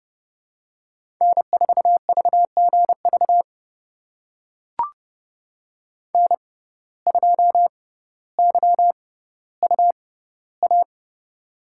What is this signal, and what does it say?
Morse code, heard as a keyed tone, about 30 wpm, spaced as for 15 wpm, 700 Hz. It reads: N4VGV N2YUA
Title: USA Call Signs - 5 Characters - 6x Character Spacing - 30wpm